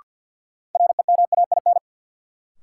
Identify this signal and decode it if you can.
Morse code, heard as a keyed tone, about 52 wpm, 700 Hz. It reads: REMAIN
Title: Sentences from Top 1200 Words - New Word Drill Alt - 50wpm